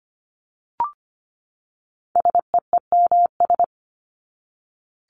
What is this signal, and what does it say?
Morse code, heard as a keyed tone, about 25 wpm, 700 Hz. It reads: SEEMS
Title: ES:S Suffix - Top 100 Words with Base Words 25wpm